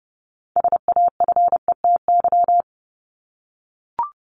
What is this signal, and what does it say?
Morse code, heard as a keyed tone, about 30 wpm, 700 Hz. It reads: SAFETY
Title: Top 501-750 Words 30wpm